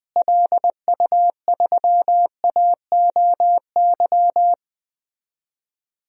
Morse code, heard as a keyed tone, about 20 wpm, 700 Hz. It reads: LU3AOY
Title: DXCC-100 Call Signs - 6 Characters - Repeated 3x - 20wpm